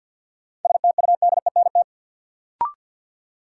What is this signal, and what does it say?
Morse code, heard as a keyed tone, about 51 wpm, 700 Hz. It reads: STUDENT
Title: Sentences from Top 1000 Words - New Word Drill - 50wpm